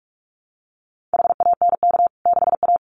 Morse code, heard as a keyed tone, about 44 wpm, 700 Hz. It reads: HANK 6A
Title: CWT Exchange 45wpm